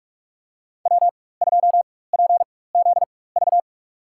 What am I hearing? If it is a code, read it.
Morse code, heard as a keyed tone, about 42 wpm, spaced as for 21 wpm, 700 Hz. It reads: W2PZV
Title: USA Call Signs - 5 Characters - 4x Character Spacing - 45wpm